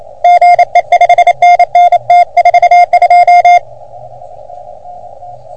Morse code, heard as a keyed tone, about 28 wpm, 710 Hz, with radio noise. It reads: GE 5NN T42
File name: OM3PA PILUP CUC 388